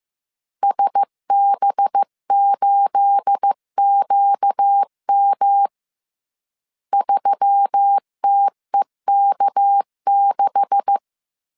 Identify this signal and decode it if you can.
Morse code, heard as a keyed tone, about 15 wpm, 800 Hz. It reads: SB8QM 3TEK6
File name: CW / DET / 1 9 1500C S B 14wpm 3302